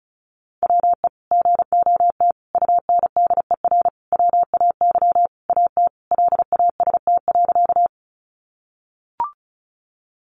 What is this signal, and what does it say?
Morse code, heard as a keyed tone, about 34 wpm, 700 Hz. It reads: WE GOT UNDER WAY AT LAST.
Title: Sentences from Top 300 Words 35wpm